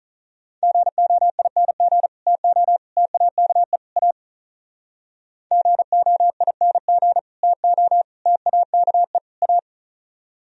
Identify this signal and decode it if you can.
Morse code, heard as a keyed tone, about 44 wpm, 700 Hz. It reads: GOING TO TAKE A GOING TO TAKE A